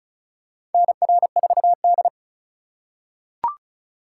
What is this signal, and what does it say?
Morse code, heard as a keyed tone, about 36 wpm, 700 Hz. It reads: NR4D
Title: USA Call Signs - 4 Characters - 4x Character Spacing - 35wpm